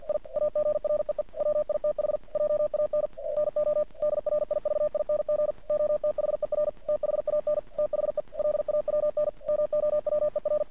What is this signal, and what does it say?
Morse code, heard as a keyed tone, about 48 wpm, 600 Hz. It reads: I AGREE WITH JAN NO DRIVING OTHER THAN THE LAWN MOWER
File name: archived / CRSnet